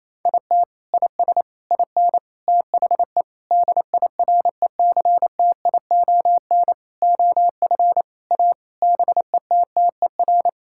Morse code, heard as a keyed tone, about 28 wpm, 700 Hz. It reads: IT IS IN THE DIRECTION OF A BETTER
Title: Sentences from Top 1000 Words - Review Part 5 - 28wpm